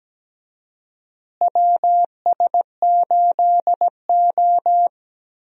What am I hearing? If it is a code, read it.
Morse code, heard as a keyed tone, about 17 wpm, 700 Hz. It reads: WS8O
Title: USA Call Signs - 4 Characters - Repeated 2x - 17wpm